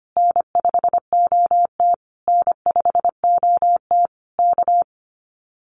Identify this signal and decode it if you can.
Morse code, heard as a keyed tone, about 25 wpm, 700 Hz. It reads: N5OT N5OT K
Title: Common QSO Phrases Encore 25wpm